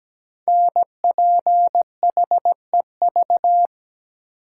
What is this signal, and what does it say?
Morse code, heard as a keyed tone, about 17 wpm, 700 Hz. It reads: NPHEV
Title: Code Groups - 5 Letters - 17wpm